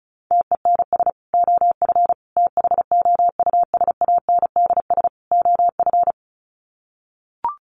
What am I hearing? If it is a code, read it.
Morse code, heard as a keyed tone, about 34 wpm, 700 Hz. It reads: TENS OF THOUSANDS OF
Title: Sets of 4 Words 35wpm